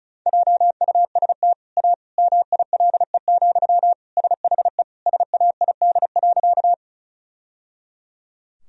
Morse code, heard as a keyed tone, about 34 wpm, 700 Hz. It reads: JUST A MILE, SHE SAID.
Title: Sentences from Top 1000 Words - Review Part 1 - 35wpm